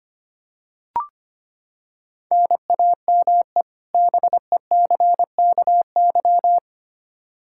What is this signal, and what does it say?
Morse code, heard as a keyed tone, about 25 wpm, 700 Hz. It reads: NAME BECKY
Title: K1USN SST QSO Practice 25wpm